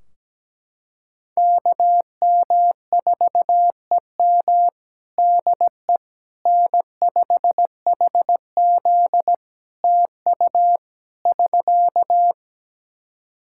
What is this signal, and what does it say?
Morse code, heard as a keyed tone, about 17 wpm, 700 Hz. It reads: KM4EM DE N5HZ TU <SK>
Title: Practice QSO - Exchange 4 - Ending - 17wpm